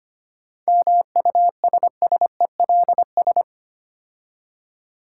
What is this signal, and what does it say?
Morse code, heard as a keyed tone, about 25 wpm, 700 Hz. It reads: MUSSELS
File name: practice / 25wpm / Difficult Words with Hints - 25wpm